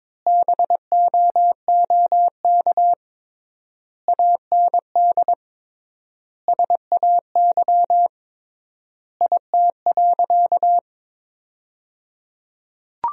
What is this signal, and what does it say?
Morse code, heard as a keyed tone, about 22 wpm, 700 Hz. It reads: BOOK AND SAY IT.
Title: 3x Word-Spacing - Sentences from Top 400 Words - 22wpm